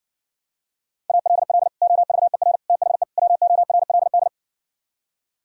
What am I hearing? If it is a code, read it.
Morse code, heard as a keyed tone, about 59 wpm, 700 Hz. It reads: ALL OVER THE WORLD